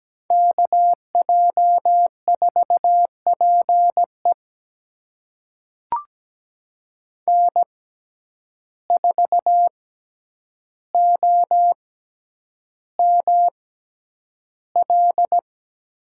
Morse code, heard as a keyed tone, about 17 wpm, spaced as for 9 wpm, 700 Hz. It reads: KJ4PE N4OML
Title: USA Call Signs - 5 Characters - 6x Character Spacing - 17wpm